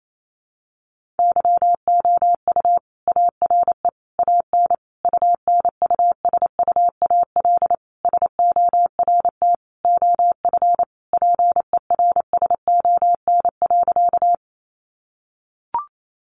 Morse code, heard as a keyed tone, about 28 wpm, 700 Hz. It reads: YOU ARE AN UNUSUAL SORT OF PERSON.
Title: Sentences from Top 1400 Words 28wpm